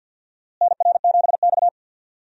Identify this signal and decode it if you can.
Morse code, heard as a keyed tone, about 51 wpm, 700 Hz. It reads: NR7X
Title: USA Call Signs - 4 Characters - Repeated 3x - 50wpm